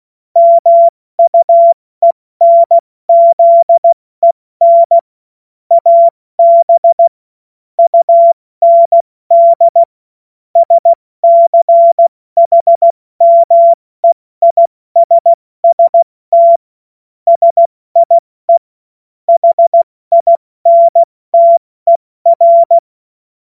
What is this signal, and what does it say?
Morse code, heard as a keyed tone, about 16 wpm, 680 Hz. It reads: MUENZEN AB UND SCHMEISST SIE HINTER
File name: Morsekurs / Witze / Witz 11. Mai 2021, 70 F80